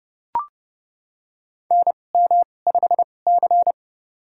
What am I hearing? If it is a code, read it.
Morse code, heard as a keyed tone, about 31 wpm, spaced as for 22 wpm, 700 Hz. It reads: NM5C
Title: USA Call Signs - 2x Character Spacing - 30wpm